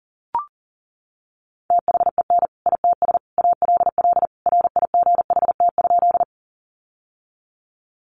Morse code, heard as a keyed tone, about 40 wpm, 700 Hz. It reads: THEN ITS ALL RIGHT?
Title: Sentences from Top 600 Words 40wpm